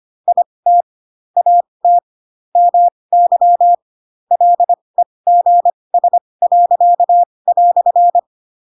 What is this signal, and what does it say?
Morse code, heard as a keyed tone, about 25 wpm, spaced as for 20 wpm, 700 Hz. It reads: IT AT MY LEGS."
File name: Pinocchio / Pinocchio ch2 15 25 ik1hge